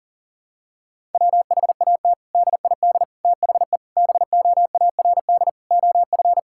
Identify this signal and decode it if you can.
Morse code, heard as a keyed tone, about 37 wpm, 700 Hz. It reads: WHAT DID THE BOARD OF
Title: Sentences from Top 1000 Words - Review Part 6 - 40wpm